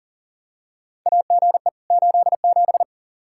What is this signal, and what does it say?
Morse code, heard as a keyed tone, about 40 wpm, 700 Hz. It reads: AGE 87
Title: K1USN SST QSO Practice 40wpm